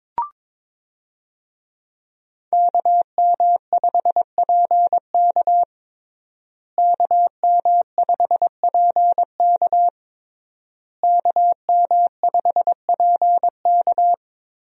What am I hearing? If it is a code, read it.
Morse code, heard as a keyed tone, about 22 wpm, 700 Hz. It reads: KM5PK KM5PK KM5PK
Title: USA Call Signs - 5 Characters - Repeated 3x - 22wpm